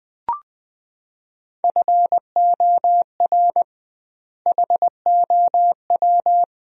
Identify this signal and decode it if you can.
Morse code, heard as a keyed tone, about 20 wpm, 700 Hz. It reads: FOR HOW
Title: 2x Word-Spacing - Sentences from Top 200 Words - 20wpm